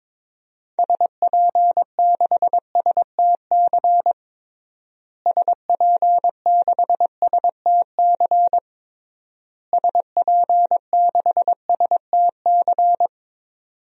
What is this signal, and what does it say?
Morse code, heard as a keyed tone, about 22 wpm, 700 Hz. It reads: SP6STC SP6STC SP6STC
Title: DXCC-100 Call Signs - 6 Characters - Repeated 3x - 22wpm